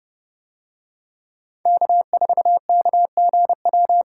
Morse code, heard as a keyed tone, about 29 wpm, 700 Hz. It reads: K4KGW